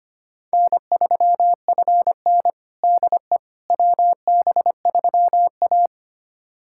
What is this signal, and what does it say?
Morse code, heard as a keyed tone, about 25 wpm, 700 Hz. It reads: N3FN DE WB3A
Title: USA Callsign DE Callsign - 4 Characters - 25wpm